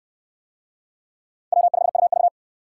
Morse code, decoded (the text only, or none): W5UV